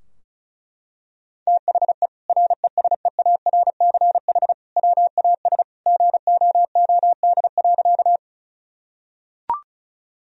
Morse code, heard as a keyed tone, about 34 wpm, 700 Hz. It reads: THE RESEARCH WAS GOOD.
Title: Sentences from Top 1000 Words - Review Part 4 - 35wpm